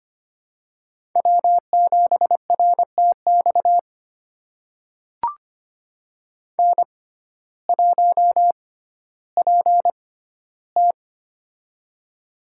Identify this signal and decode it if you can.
Morse code, heard as a keyed tone, about 25 wpm, spaced as for 14 wpm, 700 Hz. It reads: W7RTX N1PT
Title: USA Call Signs - 6x Character Spacing - 25wpm